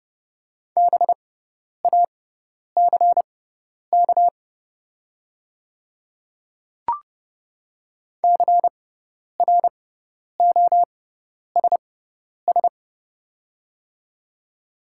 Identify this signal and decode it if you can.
Morse code, heard as a keyed tone, about 30 wpm, spaced as for 10 wpm, 700 Hz. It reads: BACK CROSS
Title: 6x Character Spacing - Top 300 Words - 30wpm